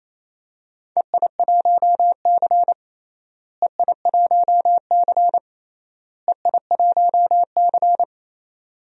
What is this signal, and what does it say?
Morse code, heard as a keyed tone, about 28 wpm, 700 Hz. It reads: EI1C EI1C EI1C
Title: DXCC-100 Call Signs - 4 Characters - Repeated 3x - 28wpm